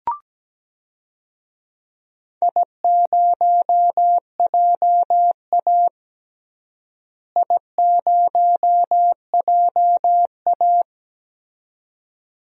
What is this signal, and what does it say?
Morse code, heard as a keyed tone, about 17 wpm, 700 Hz. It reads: I0JA I0JA
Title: DXCC-100 Call Signs - 4 Characters - Repeated 2x - 17wpm